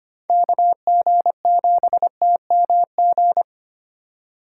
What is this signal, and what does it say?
Morse code, heard as a keyed tone, about 25 wpm, 700 Hz. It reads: KG7TMG
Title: USA Call Signs - 6 Characters - 4x Character Spacing - 25wpm